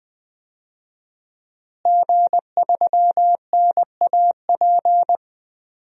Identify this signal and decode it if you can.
Morse code, heard as a keyed tone, about 20 wpm, 700 Hz. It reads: G3NAP
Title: DXCC-100 Call Signs - 5 Characters - Repeated 3x - 20wpm